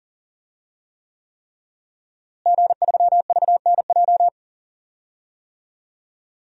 Morse code, heard as a keyed tone, about 40 wpm, 700 Hz. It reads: G3VNJ
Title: DXCC-100 Call Signs - 5 Characters - Repeated 3x - 40wpm